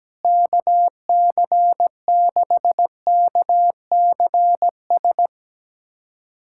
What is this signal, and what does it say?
Morse code, heard as a keyed tone, about 17 wpm, 700 Hz. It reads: KC6KCS
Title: USA Callsign DE Callsign - 6 Characters - 17wpm